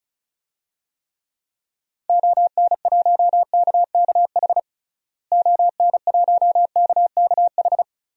Morse code, heard as a keyed tone, about 36 wpm, 700 Hz. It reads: ON1KKH ON1KKH
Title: DXCC-100 Call Signs - 6 Characters - Repeated 2x - 35wpm